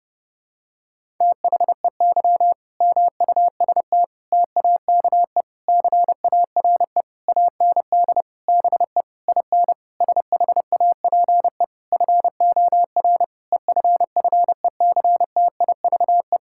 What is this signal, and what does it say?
Morse code, heard as a keyed tone, about 30 wpm, 700 Hz. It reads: THEY MUST TAKE CARE AND BE IN SHAPE FOR EFFECTIVE
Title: Sentences from Top 1000 Words - Review Part 6 - 30wpm